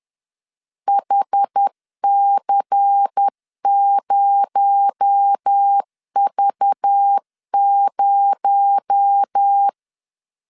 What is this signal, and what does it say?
Morse code, heard as a keyed tone, about 10 wpm, 800 Hz. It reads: HC0V0
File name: LEZIONI / 6 PKX0 / 6 PKX0 10wpm 2707